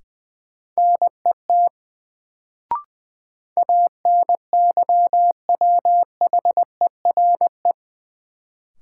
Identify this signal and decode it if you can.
Morse code, heard as a keyed tone, about 20 wpm, 700 Hz. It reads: NET ANYWHERE
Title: Sentences from Top 1400 Words - New Word Drill - 20wpm